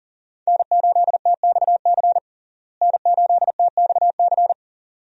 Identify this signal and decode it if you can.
Morse code, heard as a keyed tone, about 40 wpm, 700 Hz. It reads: N8TXC N8TXC
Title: USA Call Signs - 5 Characters - Repeated 2x - 40wpm